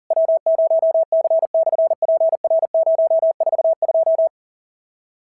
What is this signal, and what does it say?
Morse code, heard as a keyed tone, about 40 wpm, 690 Hz. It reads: W0C/PR042
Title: SOTA QSO Practice 40wpm